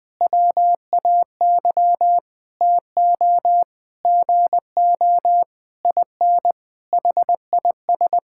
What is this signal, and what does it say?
Morse code, heard as a keyed tone, about 20 wpm, 700 Hz. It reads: WAY TO GO IN HIS
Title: Sentences with 7-Word Phrases - 20wpm